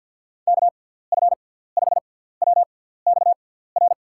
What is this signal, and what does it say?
Morse code, heard as a keyed tone, about 51 wpm, spaced as for 17 wpm, 700 Hz. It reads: KF5WXR
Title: USA Call Signs - 6 Characters - 6x Character Spacing - 50wpm